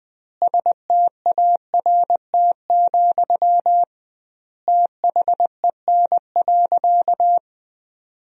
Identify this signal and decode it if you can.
Morse code, heard as a keyed tone, about 21 wpm, 700 Hz. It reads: START, THEN.
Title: 2x Word-Spacing - Sentences from Top 300 Words - 20wpm